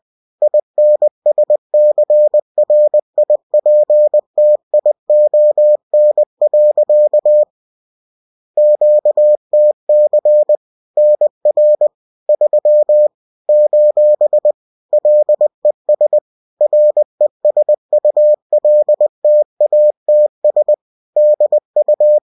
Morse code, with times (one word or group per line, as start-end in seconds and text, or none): 0.4-7.4 INSCRIPTION.
8.6-10.6 QTC
11.0-11.9 NR
12.3-13.1 3
13.5-14.5 :
14.9-16.2 LES
16.6-20.7 RESULTATS
21.2-22.2 DU